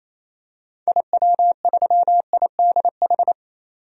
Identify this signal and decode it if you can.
Morse code, heard as a keyed tone, about 28 wpm, 700 Hz. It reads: IW3IDH
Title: DXCC-100 Call Signs - 6 Characters - 28wpm